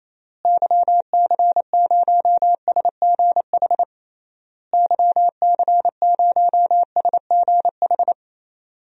Morse code, heard as a keyed tone, about 28 wpm, 700 Hz. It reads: YC0SGH YC0SGH
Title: DXCC-100 Call Signs - 6 Characters - Repeated 3x - 28wpm